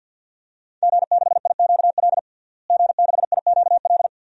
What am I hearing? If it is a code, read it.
Morse code, heard as a keyed tone, about 50 wpm, 700 Hz. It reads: G6IQL G6IQL
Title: DXCC-100 Call Signs - 5 Characters - Repeated 2x - 50wpm